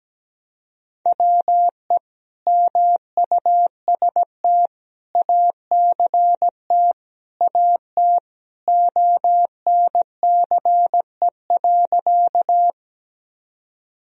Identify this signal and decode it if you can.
Morse code, heard as a keyed tone, about 17 wpm, 700 Hz. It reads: WE MUST ACT AT ONCE.
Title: Sentences from Top 1000 Words - Review Part 1 - 17wpm